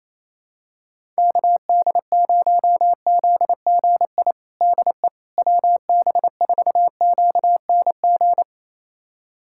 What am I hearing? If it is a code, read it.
Morse code, heard as a keyed tone, about 28 wpm, 700 Hz. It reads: KD0ZGI DE WB4QNG
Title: USA Callsign DE Callsign - 6 Characters - 28wpm